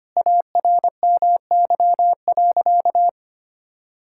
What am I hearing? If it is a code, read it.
Morse code, heard as a keyed tone, about 25 wpm, 700 Hz. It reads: ARMY.
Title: Sentences from Top 1000 Words - Review Part 5 - 25wpm